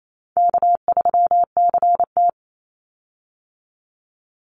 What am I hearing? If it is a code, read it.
Morse code, heard as a keyed tone, about 28 wpm, 700 Hz. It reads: K3CT
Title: Easy Call Signs - 28wpm